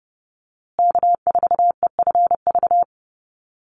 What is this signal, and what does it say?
Morse code, heard as a keyed tone, about 30 wpm, 700 Hz. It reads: K4EFV
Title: USA Call Signs Repeated 3x 30wpm